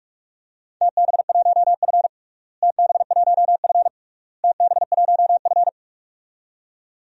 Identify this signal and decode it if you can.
Morse code, heard as a keyed tone, about 42 wpm, 700 Hz. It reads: TB1F TB1F TB1F
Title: DXCC-100 Call Signs - 4 Characters - Repeated 3x - 45wpm